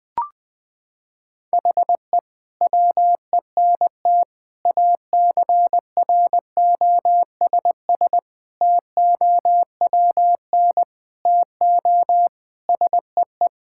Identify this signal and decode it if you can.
Morse code, heard as a keyed tone, about 20 wpm, 700 Hz. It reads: HE WENT ACROSS TOWN TO SEE
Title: Sentences from Top 1000 Words - Review Part 5 - 20wpm